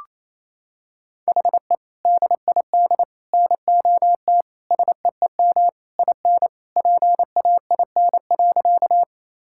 Morse code, heard as a keyed tone, about 27 wpm, 700 Hz. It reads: HE DID NOT SEEM IN PAIN.
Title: Sentences from Top 1600 Words 28wpm